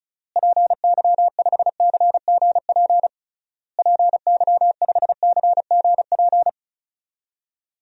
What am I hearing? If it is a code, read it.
Morse code, heard as a keyed tone, about 36 wpm, 700 Hz. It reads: PY5CGP PY5CGP
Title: DXCC-100 Call Signs - 6 Characters - Repeated 3x - 35wpm